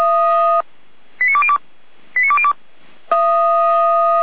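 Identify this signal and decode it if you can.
Five-tone selective call (ZVEI), heard as quick runs of five tones, tones of about 70 ms. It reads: 89202 89202
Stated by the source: ZVEI five-tone call 89230 received by SDR, feuerwehr-orlamuende.org fire brigade audio archive.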